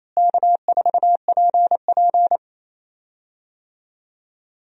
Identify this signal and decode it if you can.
Morse code, heard as a keyed tone, about 28 wpm, 700 Hz. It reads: K4PP
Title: USA Call Signs - 4 Characters - Repeated 3x - 28wpm